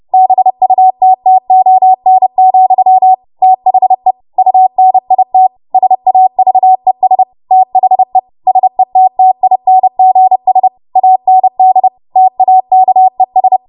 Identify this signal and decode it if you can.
Morse code, heard as a keyed tone, about 30 wpm, 750 Hz. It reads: BUTTON, THE UNIT SAVES THE SETTINGS AND TAKES
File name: Morse / 30WPM